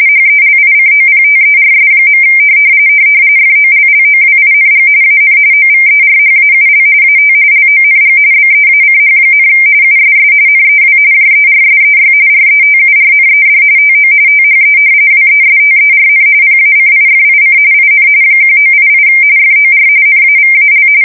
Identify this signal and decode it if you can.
RTTY, about 45.45 baud. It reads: HELICOPTERS. FOLLOWING AN OVERHAUL AT LONG BEACH, USS NEW JERSEY RETURNED TO THE WESTERN PACIFIC IN 1988 FOR NAVAL
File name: fsk170 / battleshipnj